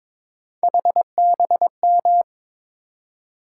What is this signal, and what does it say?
Morse code, heard as a keyed tone, about 22 wpm, 700 Hz. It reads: HBM
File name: practice / 22wpm / Three Letter-Number Combinations - 22wpm